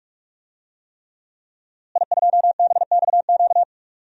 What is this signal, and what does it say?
Morse code, heard as a keyed tone, about 46 wpm, 700 Hz. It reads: I2BXQ
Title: DXCC-100 Call Signs - 5 Characters - Repeated 3x - 45wpm